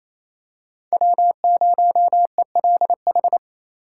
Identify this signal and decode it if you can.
Morse code, heard as a keyed tone, about 28 wpm, 700 Hz. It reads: W0ELH
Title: USA Call Signs - 5 Characters - 2x Character Spacing - 28wpm